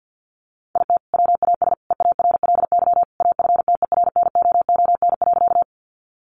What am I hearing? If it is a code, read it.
Morse code, heard as a keyed tone, about 50 wpm, 700 Hz. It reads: IT WAS EARLY AFTERNOON.
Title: Sentences from Top 1500 Words 50wpm